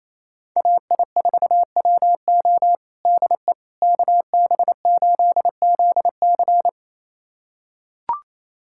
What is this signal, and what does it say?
Morse code, heard as a keyed tone, about 28 wpm, 700 Hz. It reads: AI4WO DE KB8ZC
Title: USA Callsign DE Callsign - 5 Characters - 28wpm